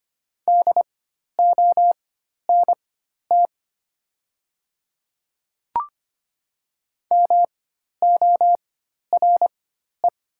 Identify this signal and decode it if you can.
Morse code, heard as a keyed tone, about 25 wpm, spaced as for 12 wpm, 700 Hz. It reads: DONT MORE
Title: 4x Character Spacing - Top 300 Words - 25wpm